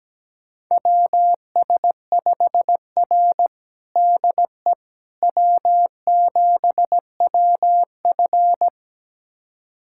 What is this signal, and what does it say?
Morse code, heard as a keyed tone, about 17 wpm, 700 Hz. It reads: WS5R DE W7WF
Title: USA Callsign DE Callsign - 4 Characters - 17wpm